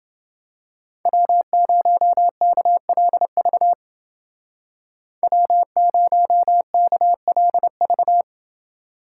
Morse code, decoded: W0KLV W0KLV

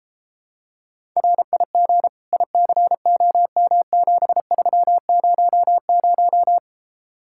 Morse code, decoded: RIG ICOM7300

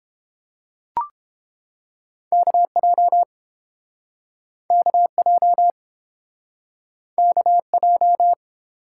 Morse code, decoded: KJ KJ KJ

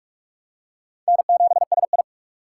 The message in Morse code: N7SI

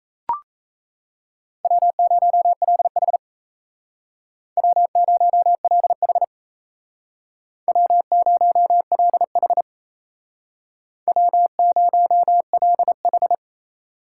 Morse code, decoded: W0LH W0LH W0LH W0LH

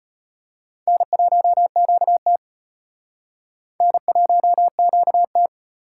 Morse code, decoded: N1QT N1QT

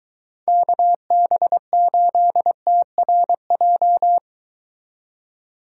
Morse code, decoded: KB8TRJ

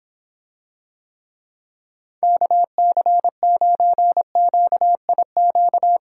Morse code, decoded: KC9QIQ

